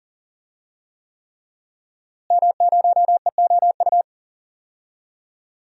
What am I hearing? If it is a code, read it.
Morse code, heard as a keyed tone, about 40 wpm, 700 Hz. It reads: M0EOU
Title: DXCC-100 Call Signs - 5 Characters - Repeated 3x - 40wpm